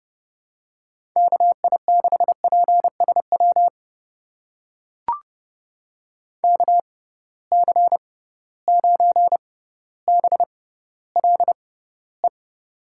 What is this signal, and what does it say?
Morse code, heard as a keyed tone, about 29 wpm, spaced as for 15 wpm, 700 Hz. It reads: KI6PSW KC9BLE
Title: USA Call Signs - 6 Characters - 6x Character Spacing - 30wpm